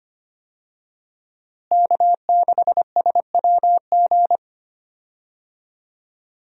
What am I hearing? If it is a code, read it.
Morse code, heard as a keyed tone, about 25 wpm, 700 Hz. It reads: K6SWG
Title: USA Call Signs - 5 Characters - Repeated 3x - 25wpm